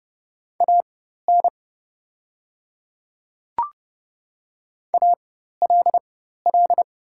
Morse code, heard as a keyed tone, about 30 wpm, spaced as for 14 wpm, 690 Hz. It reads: AN ALL
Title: 4x Character Spacing - Top 100 Words - 30wpm